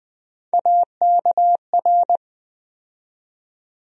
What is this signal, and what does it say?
Morse code, heard as a keyed tone, about 20 wpm, 700 Hz. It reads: AKR
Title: USA Call Sign Postfixes - 3 Letters 20wpm